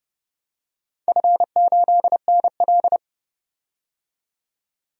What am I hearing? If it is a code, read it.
Morse code, heard as a keyed tone, about 29 wpm, 700 Hz. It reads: F8NL
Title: DXCC-100 Call Signs - 4 Characters - 30wpm